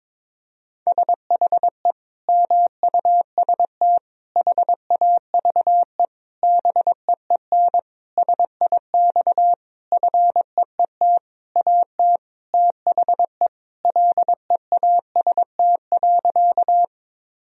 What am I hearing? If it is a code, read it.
Morse code, heard as a keyed tone, about 22 wpm, 700 Hz. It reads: SHE MUST HAVE BEEN SIX FEET AT THE LEAST.
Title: Sentences from Top 1000 Words - Review Part 2 - 22wpm